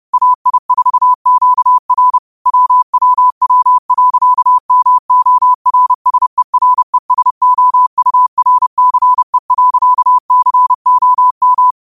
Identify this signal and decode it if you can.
Morse code, heard as a keyed tone, about 30 wpm, 1000 Hz. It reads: AI4QR WWW.MORSERESOURCE.COM
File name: qotdfiles / 30wpm